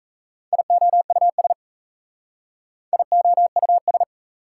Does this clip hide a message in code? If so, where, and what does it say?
Yes: Morse code from 0.5 to 4.0 s, heard as a keyed tone, about 41 wpm, 700 Hz. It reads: IOUS IOUS